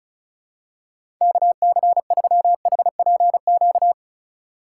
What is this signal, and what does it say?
Morse code, heard as a keyed tone, about 36 wpm, 700 Hz. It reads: KC3HPQ
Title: USA Call Signs - 6 Characters - 3x Character Spacing - 35wpm